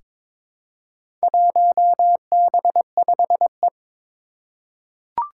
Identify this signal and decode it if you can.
Morse code, heard as a keyed tone, about 22 wpm, 700 Hz. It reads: 1B5E
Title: Four Challenging Letter-Number Combinations 22wpm